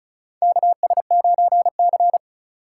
Morse code, heard as a keyed tone, about 36 wpm, 700 Hz. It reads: KS9C
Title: USA Call Signs - 3x Character Spacing - 35wpm